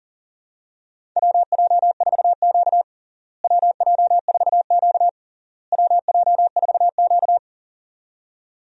Morse code, heard as a keyed tone, about 37 wpm, 700 Hz. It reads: WJ4Q WJ4Q WJ4Q
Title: USA Call Signs - 4 Characters - Repeated 3x - 40wpm